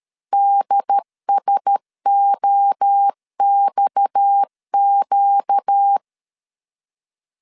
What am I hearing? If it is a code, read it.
Morse code, heard as a keyed tone, about 13 wpm, 800 Hz. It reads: DSOXQ
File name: CW / DET / 1 6 1500C S A 12wpm 3440